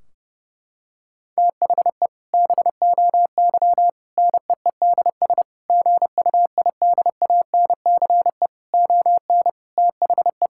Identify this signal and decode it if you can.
Morse code, heard as a keyed tone, about 32 wpm, 700 Hz. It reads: THE BOY NEEDS GUIDANCE ON THE
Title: Sentences from Top 1600 Words 30wpm